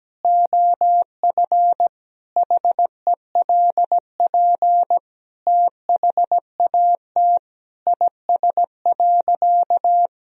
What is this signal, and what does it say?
Morse code, heard as a keyed tone, about 17 wpm, 700 Hz. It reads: OF HELP THAT IS.
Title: Sentences from Top 1000 Words - Review Part 4 - 17wpm